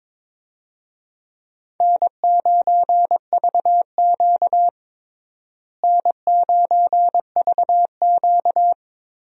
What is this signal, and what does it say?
Morse code, heard as a keyed tone, about 22 wpm, 700 Hz. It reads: N9VQ N9VQ
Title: USA Call Signs - 4 Characters - Repeated 3x - 22wpm